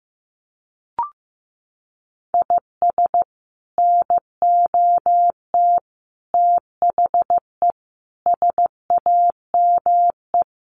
Morse code, heard as a keyed tone, about 15 wpm, 700 Hz. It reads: IS NOT THE SAME
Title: Sets of 5 Words Encore 15wpm